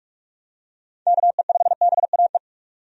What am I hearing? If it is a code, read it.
Morse code, heard as a keyed tone, about 43 wpm, 700 Hz. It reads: KE5BAE
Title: USA Call Signs - 6 Characters - 45wpm